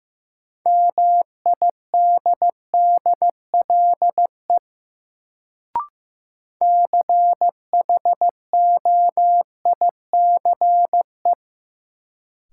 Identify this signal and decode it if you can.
Morse code, heard as a keyed tone, about 15 wpm, 700 Hz. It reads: MIDDLE CHOICE
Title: Sentences from Top 900 Words - New Word Drill - 15wpm